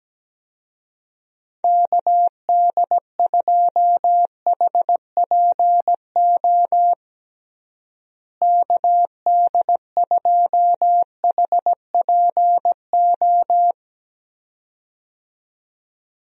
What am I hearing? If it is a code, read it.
Morse code, heard as a keyed tone, about 17 wpm, 700 Hz. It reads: KD2HPO KD2HPO
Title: USA Call Signs - 6 Characters - Repeated 2x - 17wpm